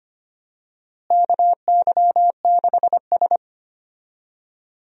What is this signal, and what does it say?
Morse code, heard as a keyed tone, about 25 wpm, 700 Hz. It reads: KY6S
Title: USA Call Signs - 4 Characters - Repeated 3x - 25wpm